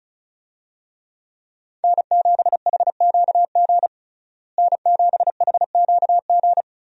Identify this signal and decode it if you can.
Morse code, heard as a keyed tone, about 36 wpm, 700 Hz. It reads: N7HQG N7HQG
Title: USA Call Signs - 5 Characters - Repeated 3x - 35wpm